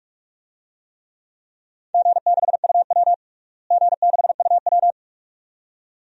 Morse code, heard as a keyed tone, about 46 wpm, 700 Hz. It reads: G6UW G6UW
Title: DXCC-100 Call Signs - 4 Characters - Repeated 2x - 45wpm